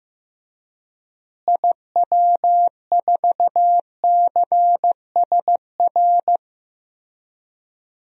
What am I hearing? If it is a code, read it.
Morse code, heard as a keyed tone, about 15 wpm, 700 Hz. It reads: IW4CSR
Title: DXCC-100 Call Signs - 6 Characters - Repeated 3x - 15wpm